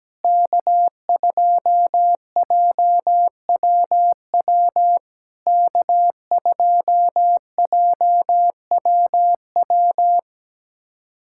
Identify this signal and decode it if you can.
Morse code, heard as a keyed tone, about 17 wpm, 700 Hz. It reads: K2JWW K2JWW
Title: SOTA QSO Practice 17wpm